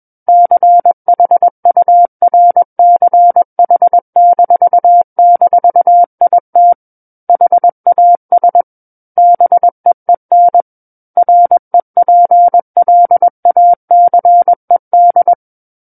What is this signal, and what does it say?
Morse code, heard as a keyed tone, about 21 wpm, 700 Hz. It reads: CHURCH--IT HAS BEEN REPLACED
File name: Book One / 12-Chapter 12 - 21 WPM